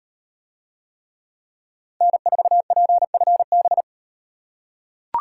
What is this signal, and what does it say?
Morse code, heard as a keyed tone, about 38 wpm, 700 Hz. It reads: N4PFB